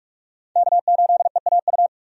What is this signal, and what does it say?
Morse code, heard as a keyed tone, about 46 wpm, 700 Hz. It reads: K8EAU
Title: USA Call Signs - 5 Characters - Repeated 2x - 45wpm